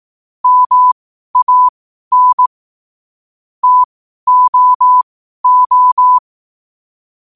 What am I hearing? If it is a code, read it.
Morse code, heard as a keyed tone, about 18 wpm, spaced as for 12 wpm, 1000 Hz. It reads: MAN TOO
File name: qotdfiles / 12wpm